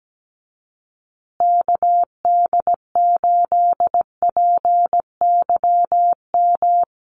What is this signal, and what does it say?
Morse code, heard as a keyed tone, about 17 wpm, 700 Hz. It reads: KD8PYM
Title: USA Call Signs Repeated 2x 17wpm